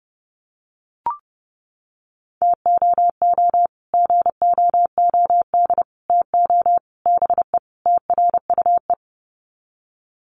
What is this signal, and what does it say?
Morse code, heard as a keyed tone, about 30 wpm, 700 Hz. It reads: TOO GOOD TO BE TRUE
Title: Sets of 5 Words Encore 30wpm